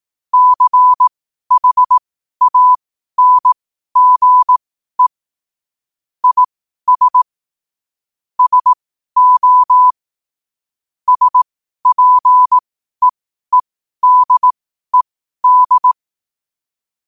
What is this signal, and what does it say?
Morse code, heard as a keyed tone, about 18 wpm, spaced as for 12 wpm, 1000 Hz. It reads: CHANGE IS SO SPEEDED
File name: qotdfiles / 12wpm